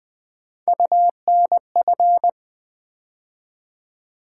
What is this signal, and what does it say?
Morse code, heard as a keyed tone, about 20 wpm, 700 Hz. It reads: UNF
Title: Build A Word - 3 Letters At A Time - 6-Letter Words - 20wpm